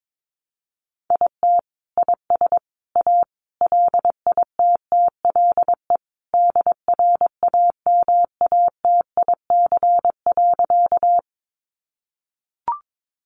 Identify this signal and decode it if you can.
Morse code, heard as a keyed tone, about 22 wpm, 700 Hz. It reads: IT IS A LITTLE DRAMATIC.
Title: Sentences from Top 1300 Words 22wpm